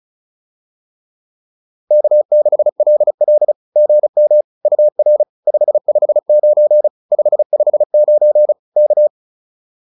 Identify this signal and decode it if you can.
Morse code, heard as a keyed tone, about 36 wpm, 600 Hz. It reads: K6LL GM UR 559 559 K